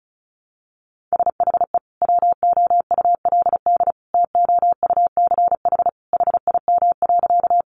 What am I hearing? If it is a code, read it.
Morse code, heard as a keyed tone, about 34 wpm, 700 Hz. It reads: SHE WOULD TOUCH HIM.
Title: Sentences from Top 1200 Words 35wpm